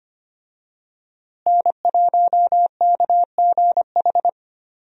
Morse code, heard as a keyed tone, about 25 wpm, 700 Hz. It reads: N1KGH
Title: USA Call Signs - 5 Characters - Repeated 2x - 25wpm